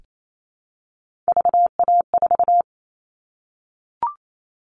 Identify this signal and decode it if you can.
Morse code, heard as a keyed tone, about 28 wpm, 700 Hz. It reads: VA4
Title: All Cut Numbers - 3-Digit 28wpm